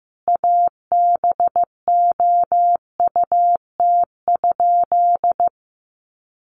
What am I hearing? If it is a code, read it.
Morse code, heard as a keyed tone, about 15 wpm, 700 Hz. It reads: ABOUT?
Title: Sentences from Top 100 Words 15wpm